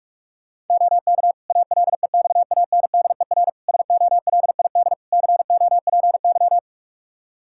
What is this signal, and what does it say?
Morse code, heard as a keyed tone, about 44 wpm, 700 Hz. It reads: OK ALEXANDER SOLID COPY
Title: K1USN SST QSO Practice 45wpm